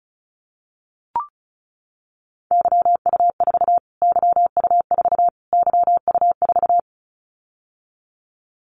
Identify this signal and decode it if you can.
Morse code, heard as a keyed tone, about 34 wpm, 700 Hz. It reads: YU4 YU4 YU4
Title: Callsign Prefixes for Top 25 DXCC - Repeated 3x - 35wpm